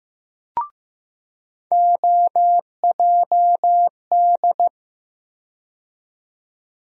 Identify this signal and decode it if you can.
Morse code, heard as a keyed tone, about 15 wpm, 700 Hz. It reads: OJD
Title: USA Call Sign Postfixes - 3 Letters 15wpm